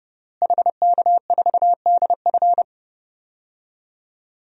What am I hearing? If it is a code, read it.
Morse code, heard as a keyed tone, about 32 wpm, 700 Hz. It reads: HK4DF
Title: DXCC-100 Call Signs - 5 Characters - Repeated 2x - 30wpm